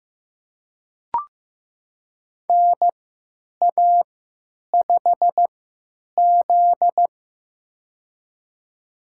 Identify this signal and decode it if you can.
Morse code, heard as a keyed tone, about 15 wpm, spaced as for 8 wpm, 700 Hz. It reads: NA5Z
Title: USA Call Signs - 3x Character Spacing - 15wpm